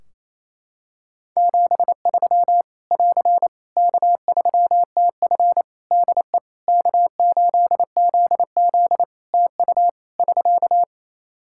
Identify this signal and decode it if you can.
Morse code, heard as a keyed tone, about 28 wpm, 700 Hz. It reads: 73 <AR> K3TF DE K8ZZ TU <SK>
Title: K1USN SST QSO Practice 28wpm